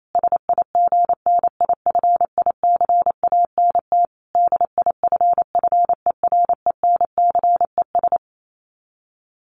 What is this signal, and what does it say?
Morse code, heard as a keyed tone, about 28 wpm, 700 Hz. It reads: SIGNIFICANT DIFFERENCES
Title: Sets of 5 Words Encore2 28wpm